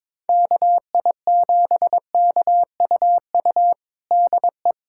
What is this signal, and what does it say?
Morse code, heard as a keyed tone, about 22 wpm, 700 Hz. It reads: KI7KUU DE
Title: USA Callsign DE Callsign 22wpm